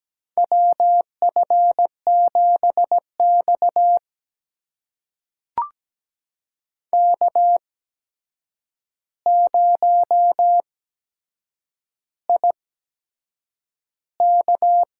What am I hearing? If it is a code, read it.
Morse code, heard as a keyed tone, about 17 wpm, spaced as for 7 wpm, 700 Hz. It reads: WF7X K0IK
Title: USA Call Signs - 4 Characters - 8x Character Spacing - 17wpm